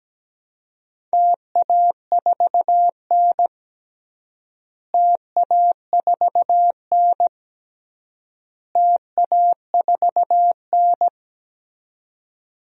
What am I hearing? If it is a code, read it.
Morse code, heard as a keyed tone, about 17 wpm, 700 Hz. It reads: TA4N TA4N TA4N
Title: DXCC-100 Call Signs - 4 Characters - Repeated 3x - 17wpm